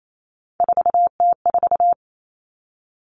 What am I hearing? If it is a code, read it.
Morse code, heard as a keyed tone, about 28 wpm, 700 Hz. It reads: 4T4
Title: Common Cut Numbers - 3-Digit 28wpm